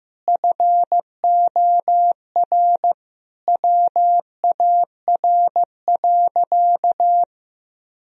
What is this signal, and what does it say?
Morse code, heard as a keyed tone, about 15 wpm, 700 Hz. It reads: FOR WAR.
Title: Sentences from Top 1000 Words - Review Part 3 - 15wpm